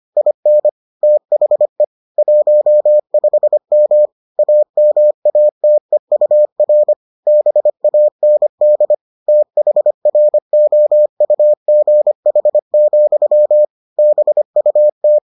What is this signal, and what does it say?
Morse code, heard as a keyed tone, about 25 wpm, 600 Hz. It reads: IN THE 15M AMATEUR BAND THROUGH, BUT